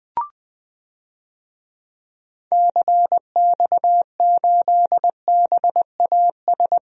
Morse code, heard as a keyed tone, about 20 wpm, 700 Hz. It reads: CX8BAS
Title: DXCC-100 Call Signs - 6 Characters - 20wpm